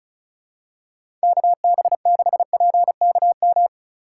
Morse code, heard as a keyed tone, about 36 wpm, 700 Hz. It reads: KB6PKM
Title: USA Call Signs - 4x Character Spacing - 35wpm